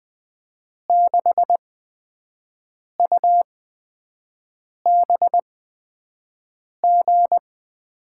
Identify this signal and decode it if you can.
Morse code, heard as a keyed tone, about 20 wpm, spaced as for 5 wpm, 700 Hz. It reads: 6UBG
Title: USA Call Signs - 6 Characters - 8x Character Spacing - 20wpm